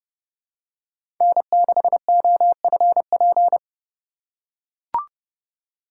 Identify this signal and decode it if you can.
Morse code, heard as a keyed tone, about 30 wpm, 700 Hz. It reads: N6OFP
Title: USA Call Signs - 5 Characters - 2x Character Spacing - 30wpm